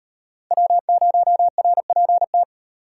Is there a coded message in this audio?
Yes: Morse code from 0.5 to 2.4 s, heard as a keyed tone, about 38 wpm, 700 Hz. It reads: W0RPT